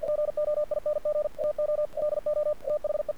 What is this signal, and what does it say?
Morse code, heard as a keyed tone, about 50 wpm, 600 Hz. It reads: GOING TO DO THE
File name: archived / CRSnet